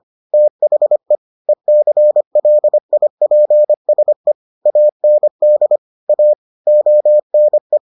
Morse code, heard as a keyed tone, about 25 wpm, 600 Hz. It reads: THE ECLIPSE AND A ONE